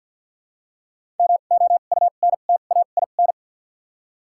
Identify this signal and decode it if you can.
Morse code, heard as a keyed tone, about 49 wpm, spaced as for 36 wpm, 700 Hz. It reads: MOUNTAIN
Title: 2x Character Spacing - Top 300 Words - 50wpm